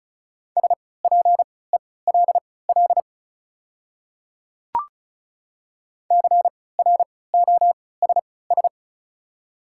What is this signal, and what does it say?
Morse code, heard as a keyed tone, about 36 wpm, spaced as for 20 wpm, 700 Hz. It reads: SPELL CROSS
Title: 3x Character Spacing - Top 300 Words - 35wpm